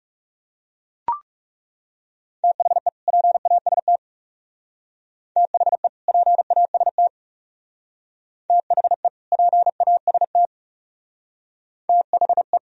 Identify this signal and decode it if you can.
Morse code, heard as a keyed tone, about 36 wpm, 700 Hz. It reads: THE PAST THE PAST THE PAST THE
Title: Speed-Racing - Sets of 2 Words 30wpm